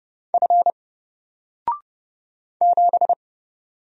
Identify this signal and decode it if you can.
Morse code, heard as a keyed tone, about 30 wpm, 700 Hz. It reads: F 7
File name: Single-Letter-Number-30wpm